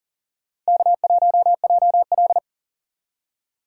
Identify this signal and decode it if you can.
Morse code, heard as a keyed tone, about 52 wpm, 700 Hz. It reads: K1JL